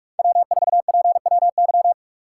Morse code, heard as a keyed tone, about 46 wpm, 700 Hz. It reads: W4PWY
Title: USA Call Signs - 5 Characters - 6x Character Spacing - 45wpm